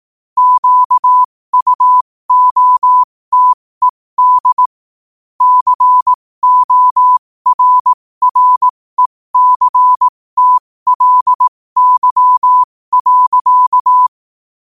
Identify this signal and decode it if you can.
Morse code, heard as a keyed tone, about 18 wpm, spaced as for 15 wpm, 1000 Hz. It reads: QUOTED CORRECTLY.
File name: qotdfiles / 15wpm